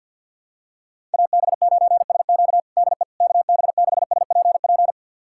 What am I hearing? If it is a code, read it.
Morse code, heard as a keyed tone, about 51 wpm, 700 Hz. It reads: AB9SQ DE KB6SPP
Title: USA Callsign DE Callsign 50wpm